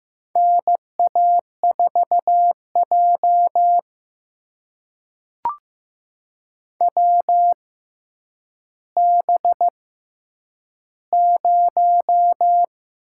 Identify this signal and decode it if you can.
Morse code, heard as a keyed tone, about 15 wpm, spaced as for 9 wpm, 700 Hz. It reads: NA4J WB0
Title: USA Call Signs - 6x Character Spacing - 15wpm